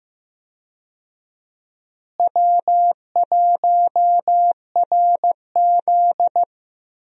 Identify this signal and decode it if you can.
Morse code, heard as a keyed tone, about 15 wpm, 700 Hz. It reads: W1RZ
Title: USA Call Signs - 4 Characters - Repeated 2x - 15wpm